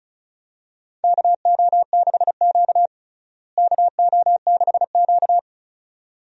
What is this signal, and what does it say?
Morse code, heard as a keyed tone, about 36 wpm, 700 Hz. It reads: KO6Q KO6Q
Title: USA Call Signs - 4 Characters - Repeated 2x - 35wpm